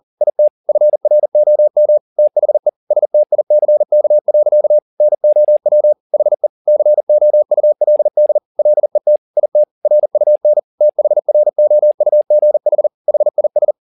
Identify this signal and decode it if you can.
Morse code, heard as a keyed tone, about 40 wpm, 600 Hz. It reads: IT FROM THE STICK. NOW HE COULD LET IT RUN THROUGH HIS